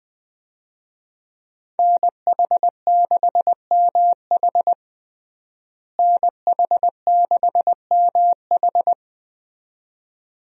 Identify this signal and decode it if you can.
Morse code, heard as a keyed tone, about 20 wpm, 700 Hz. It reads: NH6MH NH6MH
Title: USA Call Signs - 5 Characters - Repeated 2x - 20wpm